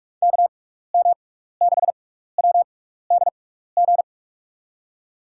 Morse code, heard as a keyed tone, about 47 wpm, spaced as for 16 wpm, 700 Hz. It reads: KM6WDG
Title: USA Call Signs - 6 Characters - 6x Character Spacing - 45wpm